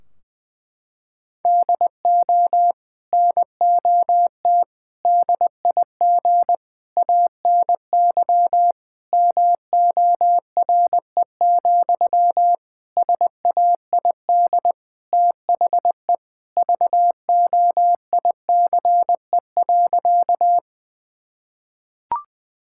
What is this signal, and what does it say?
Morse code, heard as a keyed tone, about 20 wpm, 700 Hz. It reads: DO NOT DIG ANY MORE, SAID THE VOICE.
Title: Sentences from Top 1600 Words 20wpm